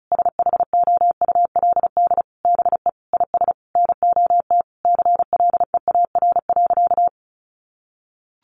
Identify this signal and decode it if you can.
Morse code, heard as a keyed tone, about 34 wpm, 700 Hz. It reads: SHOULD BE IS NOT CLEAR.
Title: Sentences from Top 1300 Words 35wpm